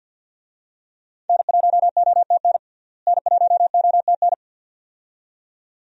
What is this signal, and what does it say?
Morse code, heard as a keyed tone, about 48 wpm, 700 Hz. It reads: N1OTN N1OTN
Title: USA Call Signs - 5 Characters - Repeated 2x - 50wpm